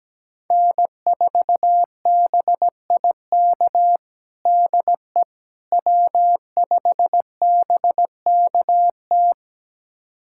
Morse code, heard as a keyed tone, about 17 wpm, 700 Hz. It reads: N4BIK DE W5BKT
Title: USA Callsign DE Callsign - 5 Characters - 17wpm